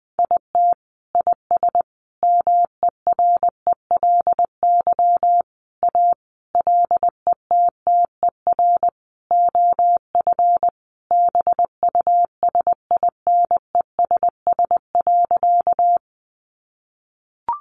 Morse code, heard as a keyed tone, about 20 wpm, 700 Hz. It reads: IT IS MERELY A LETTER OF BUSINESS.
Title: Sentences from Top 1400 Words 20wpm